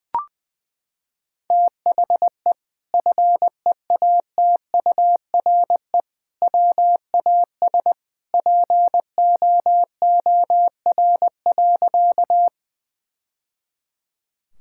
Morse code, heard as a keyed tone, about 20 wpm, 700 Hz. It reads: THE FEATURE WAS POOR.
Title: Sentences from Top 1000 Words - Review Part 5 - 20wpm